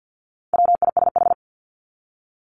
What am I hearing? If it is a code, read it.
Morse code, heard as a keyed tone, about 50 wpm, 700 Hz. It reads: WISH
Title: Top 1501-1750 Words 50wpm